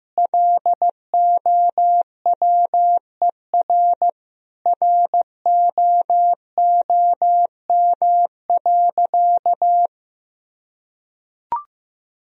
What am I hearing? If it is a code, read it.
Morse code, heard as a keyed tone, about 16 wpm, 700 Hz. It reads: LOWER ROOM.
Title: Sentences from Top 1000 Words - Review Part 4 - 15wpm